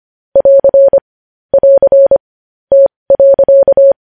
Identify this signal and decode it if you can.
Morse code, heard as a keyed tone, about 25 wpm, spaced as for 20 wpm, 550 Hz. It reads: <AR> <AR> T.
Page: news headlines in morse code at 20 wpm